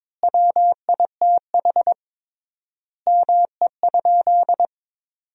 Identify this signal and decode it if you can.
Morse code, heard as a keyed tone, about 22 wpm, 700 Hz. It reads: WITH ME?
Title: 3x Word-Spacing - Sentences from Top 300 Words - 22wpm